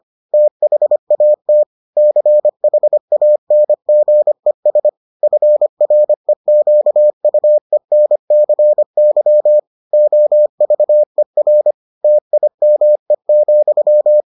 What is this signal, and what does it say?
Morse code, heard as a keyed tone, about 25 wpm, 600 Hz. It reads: THAT CHANGES FREQUENCY OVER TIME,